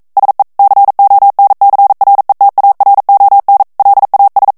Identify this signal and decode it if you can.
Morse code, heard as a keyed tone, about 42 wpm, 800 Hz. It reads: SE CONCRETARON LAS
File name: morse mp3 / ESPAGNOL TEXTE EN CLAIR 1 VITESSE 40 MOTS MINUTES